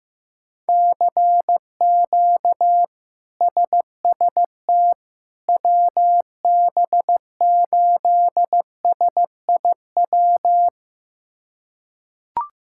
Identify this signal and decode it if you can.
Morse code, heard as a keyed tone, about 15 wpm, 700 Hz. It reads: CQ SST WB8SIW
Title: K1USN SST QSO Practice - 15wpm